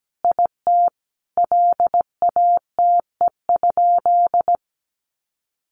Morse code, heard as a keyed tone, about 17 wpm, 700 Hz. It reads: IT LATE?
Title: Sentences from Top 300 Words 17wpm